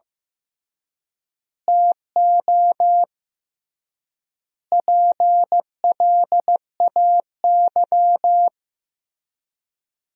Morse code, heard as a keyed tone, about 16 wpm, 700 Hz. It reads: TO PLAY
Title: 3x Word-Spacing - Sentences from Top 300 Words - 15wpm